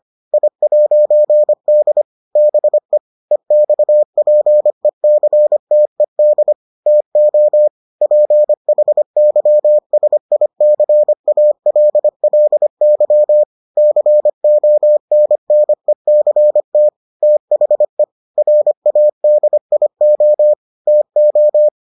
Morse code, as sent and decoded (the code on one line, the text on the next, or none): .. .----. -.. / -... / . -..- .--. -.-. - . -.. / - --- / .--. .... -.-- ... .. -.-. .- .-.. .-.. -.-- / -.-. --- -. -. -.-. - / - .... / .-. .- -.. .. --- / - ---
I'D BE EXPECTED TO PHYSICALLY CONNECT THE RADIO TO